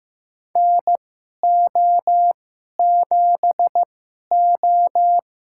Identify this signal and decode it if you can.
Morse code, heard as a keyed tone, about 15 wpm, spaced as for 11 wpm, 700 Hz. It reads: NO7O
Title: USA Call Signs - 4 Characters - 2x Character Spacing - 15wpm